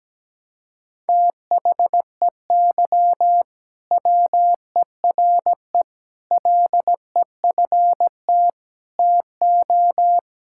Morse code, heard as a keyed tone, about 17 wpm, 700 Hz. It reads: THEY WERE LEFT TO
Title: Sentences from Top 1000 Words - Review Part 5 - 17wpm